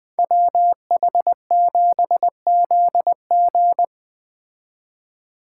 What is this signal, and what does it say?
Morse code, heard as a keyed tone, about 20 wpm, 700 Hz. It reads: WH7ZG
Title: Speed Racing - USA 2x2 Call Signs 17wpm